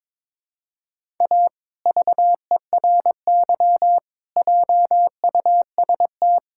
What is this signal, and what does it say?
Morse code, heard as a keyed tone, about 21 wpm, 700 Hz. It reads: A VERY JUST
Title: Sentences from Top 1600 Words 22wpm